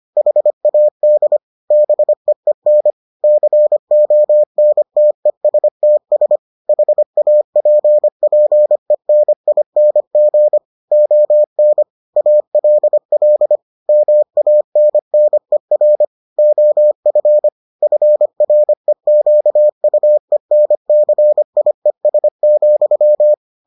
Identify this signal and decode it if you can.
Morse code, heard as a keyed tone, about 25 wpm, 600 Hz. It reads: HAD BEEN CONTESTS HAPPENING ON ALL MANNER OF FREQUENCIES,